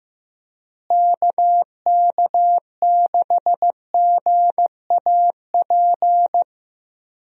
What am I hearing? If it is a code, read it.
Morse code, heard as a keyed tone, about 15 wpm, 700 Hz. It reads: KK6GAP
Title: USA Call Signs - 6 Characters - Repeated 3x - 15wpm